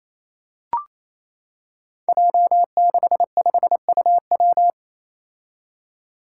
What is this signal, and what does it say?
Morse code, heard as a keyed tone, about 28 wpm, 700 Hz. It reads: J65UW
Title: Five Challenging Letter-Number Combinations 28wpm